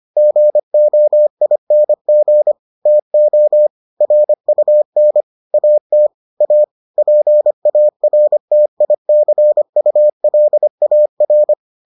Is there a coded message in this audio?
Yes: Morse code from 0.2 to 11.5 s, heard as a keyed tone, about 25 wpm, 600 Hz. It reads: GOING TO RUN AT A PARTICULAR